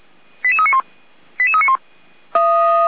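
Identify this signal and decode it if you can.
Five-tone selective call (ZVEI), heard as quick runs of five tones, tones of about 70 ms. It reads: 88391 88391
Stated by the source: ZVEI five-tone selective call received with an SDR.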